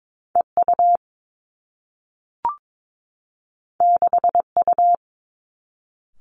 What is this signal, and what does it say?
Morse code, heard as a keyed tone, about 22 wpm, 700 Hz. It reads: E U 6 U
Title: Two Challenging Letter-Number Combinations 22wpm